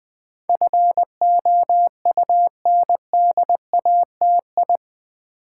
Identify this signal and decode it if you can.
Morse code, heard as a keyed tone, about 20 wpm, 700 Hz. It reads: FOUNDATI
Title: Build A Word - 4 Letters At A Time - 10-Letter Words - 20wpm